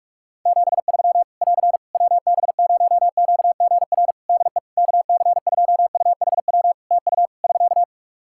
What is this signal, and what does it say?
Morse code, heard as a keyed tone, about 46 wpm, 700 Hz. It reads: 73 <AR> WB0QGR DE KC2UHW TU <SK>
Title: Practice QSO - Exchange 4 - Ending - 45wpm